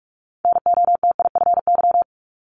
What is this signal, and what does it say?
Morse code, heard as a keyed tone, about 46 wpm, 700 Hz. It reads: NOTIFY
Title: Multi-letter Phonemes 45wpm